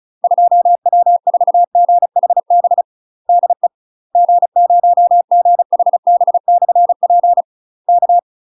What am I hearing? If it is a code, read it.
Morse code, heard as a keyed tone, about 36 wpm, 700 Hz. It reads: 2W4GHB DE G0GHB/P K
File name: UK MASTER QSOs TRAINING / 35WPM / UK MASTER QSOs 0010